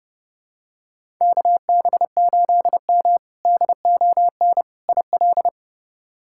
Morse code, heard as a keyed tone, about 30 wpm, 700 Hz. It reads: KB8M DON IL
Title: K1USN SST QSO Practice - 30wpm